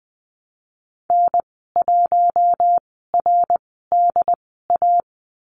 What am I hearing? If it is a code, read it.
Morse code, heard as a keyed tone, about 20 wpm, spaced as for 15 wpm, 700 Hz. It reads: N1RDA
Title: 186 Lesson 30g - Letter b - Call Signs 20wpm